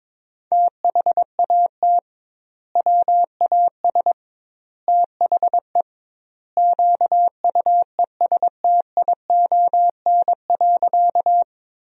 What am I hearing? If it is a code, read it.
Morse code, heard as a keyed tone, about 22 wpm, 700 Hz. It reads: THAT WAS THE QUESTION.
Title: 2x Word-Spacing - Sentences from Top 400 Words - 22wpm